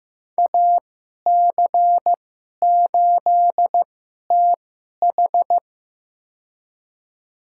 Morse code, heard as a keyed tone, about 15 wpm, spaced as for 11 wpm, 700 Hz. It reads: AC8TH
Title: USA Call Signs - 5 Characters - 2x Character Spacing - 15wpm